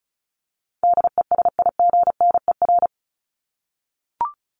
Morse code, heard as a keyed tone, about 34 wpm, 700 Hz. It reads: DESIGNER
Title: Top 1501-1750 Words 35wpm